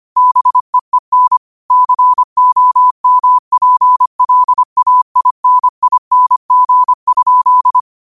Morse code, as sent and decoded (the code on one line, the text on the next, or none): -... . -. / -.-. --- -- .--. .-.. .- .. -. .. -. --. ..--..
BEEN COMPLAINING?